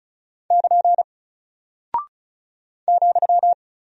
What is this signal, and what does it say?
Morse code, heard as a keyed tone, about 36 wpm, 700 Hz. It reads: <KN> ,
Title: Single Character (Letter:Number:Punctuation:Prosign) 35wpm